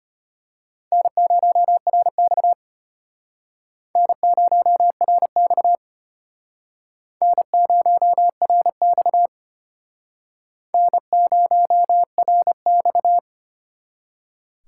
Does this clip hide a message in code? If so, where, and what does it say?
Yes: Morse code from 0.9 to 13.2 s, heard as a keyed tone, about 32 wpm, 700 Hz. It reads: N0RX N0RX N0RX N0RX